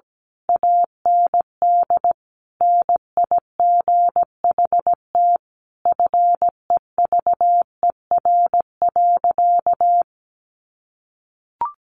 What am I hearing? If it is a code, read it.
Morse code, heard as a keyed tone, about 17 wpm, 700 Hz. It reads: AND NIGHT FEVER.
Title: A Sound Of Thunder 17wpm